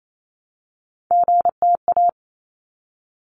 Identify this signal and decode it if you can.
Morse code, heard as a keyed tone, about 28 wpm, 700 Hz. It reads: GTA
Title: ARRL Sections 28wpm